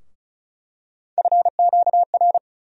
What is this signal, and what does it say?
Morse code, heard as a keyed tone, about 34 wpm, 700 Hz. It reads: FQR
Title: USA Call Sign Postfixes - 3 Letters 35wpm